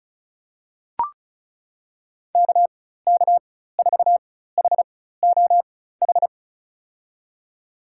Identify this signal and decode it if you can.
Morse code, heard as a keyed tone, about 36 wpm, spaced as for 16 wpm, 690 Hz. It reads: KK4HOH
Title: USA Call Signs - 6 Characters - 4x Character Spacing - 35wpm